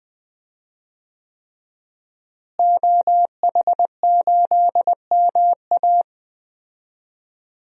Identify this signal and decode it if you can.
Morse code, heard as a keyed tone, about 20 wpm, 700 Hz. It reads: OH8MA
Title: DXCC-100 Call Signs - 5 Characters - Repeated 3x - 20wpm